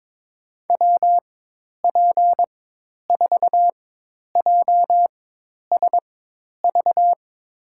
Morse code, heard as a keyed tone, about 22 wpm, spaced as for 10 wpm, 700 Hz. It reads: WP4JSV
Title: USA Call Signs - 6 Characters - 4x Character Spacing - 22wpm